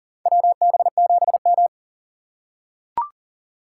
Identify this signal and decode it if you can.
Morse code, heard as a keyed tone, about 40 wpm, 700 Hz. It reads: WB7M
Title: USA Call Signs - 2x Character Spacing - 40wpm